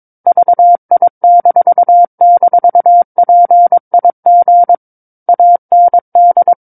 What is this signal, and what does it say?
Morse code, heard as a keyed tone, about 22 wpm, 700 Hz. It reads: VI--PIG AND
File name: 06-Chapter-06 22 WPM